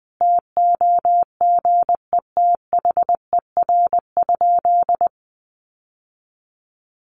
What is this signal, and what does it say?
Morse code, heard as a keyed tone, about 20 wpm, 700 Hz. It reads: TOGETHER?
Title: Sentences from Top 1400 Words 20wpm